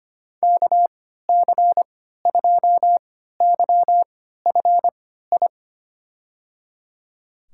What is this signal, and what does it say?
Morse code, heard as a keyed tone, about 25 wpm, spaced as for 14 wpm, 690 Hz. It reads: KC2YFI
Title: USA Call Signs - 6 Characters - 3x Character Spacing - 25wpm